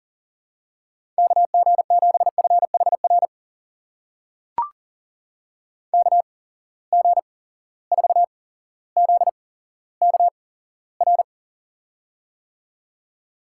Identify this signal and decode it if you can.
Morse code, heard as a keyed tone, about 37 wpm, spaced as for 17 wpm, 700 Hz. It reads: KG7FHR KG4ZKR
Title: USA Call Signs - 6 Characters - 8x Character Spacing - 40wpm